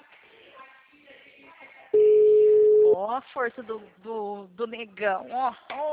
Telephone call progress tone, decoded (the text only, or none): ringback tone